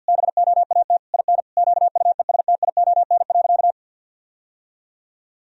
Morse code, heard as a keyed tone, about 46 wpm, 700 Hz. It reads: BOAT IN QUESTION.
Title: Sentences from Top 1000 Words - Review Part 3 - 50wpm